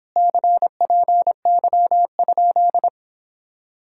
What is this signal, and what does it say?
Morse code, heard as a keyed tone, about 29 wpm, 700 Hz. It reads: CPY?